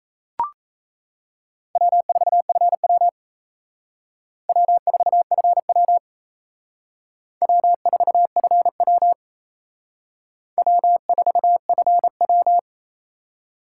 Morse code, heard as a keyed tone, about 36 wpm, 700 Hz. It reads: W4FW W4FW W4FW W4FW